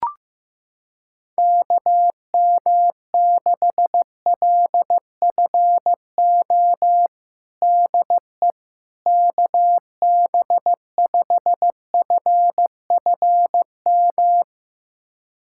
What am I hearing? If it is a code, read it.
Morse code, heard as a keyed tone, about 15 wpm, 700 Hz. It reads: KM6LFO DE KB5FFM
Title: USA Callsign DE Callsign - 6 Characters - 15wpm